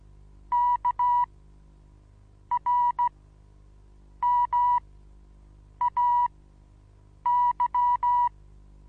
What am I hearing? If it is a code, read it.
Morse code, heard as a keyed tone, about 16 wpm, spaced as for 6 wpm, 970 Hz. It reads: KRMAY
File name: Onlinelehrgang / morsen / DJ4UF Morsen25